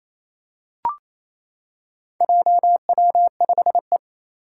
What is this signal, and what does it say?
Morse code, heard as a keyed tone, about 27 wpm, 700 Hz. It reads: JW5E
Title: Four Challenging Letter-Number Combinations 28wpm